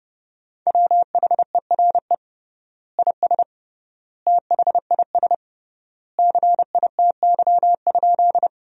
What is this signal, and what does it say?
Morse code, heard as a keyed tone, about 32 wpm, 700 Hz. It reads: WHERE IS THIS CITY?
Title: 3x Word-Spacing - Sentences from Top 300 Words - 30wpm